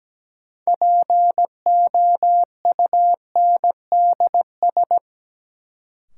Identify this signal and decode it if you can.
Morse code, heard as a keyed tone, about 17 wpm, 700 Hz. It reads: POUNDS
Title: Sentences from Top 1000 Words - Review Part 2 - New Word Drill - 17wpm